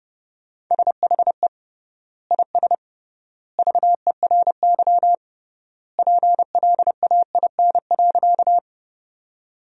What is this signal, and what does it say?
Morse code, heard as a keyed tone, about 30 wpm, 700 Hz. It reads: SHE IS VERY PLAIN.
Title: 3x Word-Spacing - Sentences from Top 400 Words - 30wpm